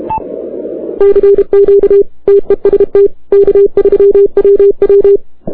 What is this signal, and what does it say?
Morse code, heard as a keyed tone, about 32 wpm, 400 Hz. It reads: CQ TEST K3WW
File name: arrl03cw 20 k3ww